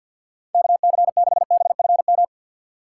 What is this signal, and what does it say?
Morse code, heard as a keyed tone, about 52 wpm, 700 Hz. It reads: KC6BFM
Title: USA Call Signs - 6 Characters - 3x Character Spacing - 50wpm